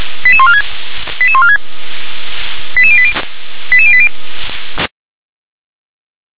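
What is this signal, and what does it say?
Five-tone selective call (ZVEI), heard as quick runs of five tones, tones of about 70 ms. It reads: 88146 88146 88080 88080
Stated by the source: ZVEI five-tone selective call received with an SDR.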